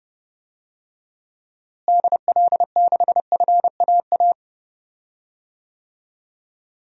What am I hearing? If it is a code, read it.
Morse code, heard as a keyed tone, about 32 wpm, 700 Hz. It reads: DL6FAA
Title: DXCC-100 Call Signs - 6 Characters - Repeated 3x - 30wpm